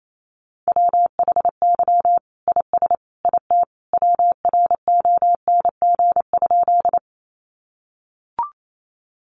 Morse code, heard as a keyed tone, about 28 wpm, 700 Hz. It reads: WHY IS IT WRONG?
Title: Sentences from Top 900 Words 28wpm